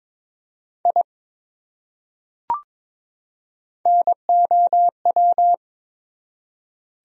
Morse code, heard as a keyed tone, about 22 wpm, 700 Hz. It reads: I NOW
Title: Top 100 Words 22wpm